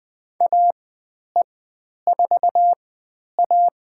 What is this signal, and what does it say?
Morse code, heard as a keyed tone, about 20 wpm, spaced as for 10 wpm, 700 Hz. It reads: AE4A
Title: 095 Lesson 16g - Letter c - Call Signs 20wpm